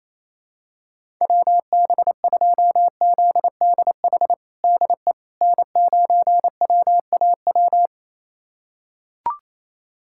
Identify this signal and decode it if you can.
Morse code, heard as a keyed tone, about 28 wpm, 700 Hz. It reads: WB2ZDH DE N9WAW
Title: USA Callsign DE Callsign 28wpm